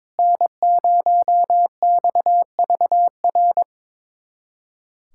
Morse code, heard as a keyed tone, about 21 wpm, 700 Hz. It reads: N0XVR
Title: USA Callsign DE Callsign - 5 Characters - 22wpm